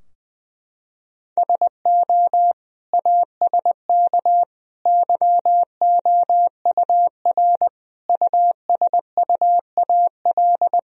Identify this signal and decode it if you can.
Morse code, heard as a keyed tone, about 20 wpm, 700 Hz. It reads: SO ASK YOUR USUAL
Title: Sentences from Top 1000 Words - Review Part 2 - 20wpm